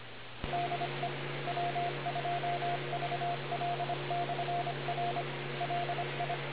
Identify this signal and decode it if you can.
Morse code, heard as a keyed tone, about 25 wpm, 700 Hz, with noise. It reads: DE W2VL/R LI